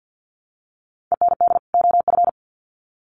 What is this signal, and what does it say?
Morse code, heard as a keyed tone, about 50 wpm, 700 Hz. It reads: END OF
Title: Sets of 2 Words 50wpm